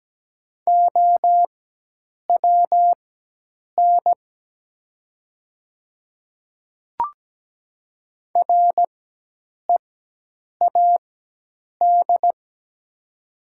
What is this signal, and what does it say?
Morse code, heard as a keyed tone, about 17 wpm, spaced as for 8 wpm, 700 Hz. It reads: OWN READ
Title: 4x Character Spacing - Top 300 Words - 17wpm